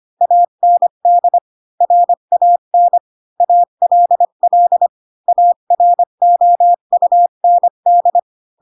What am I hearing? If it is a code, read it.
Morse code, heard as a keyed tone, about 25 wpm, 700 Hz. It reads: AND RAN ALL AROUND
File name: Pinocchio / Pinocchio ch3 20 25 ik1hge